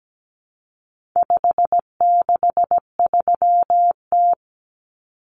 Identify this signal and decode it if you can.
Morse code, heard as a keyed tone, about 17 wpm, 700 Hz. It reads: 563T
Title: Common Cut Numbers - 4-Digit 17wpm